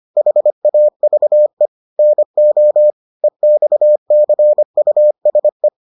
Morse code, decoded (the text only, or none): HAVE NO EXCUSE